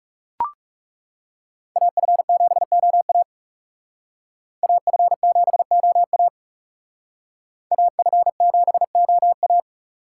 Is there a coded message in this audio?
Yes: Morse code from 1.8 to 9.6 s, heard as a keyed tone, about 39 wpm, 700 Hz. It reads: AF7OA AF7OA AF7OA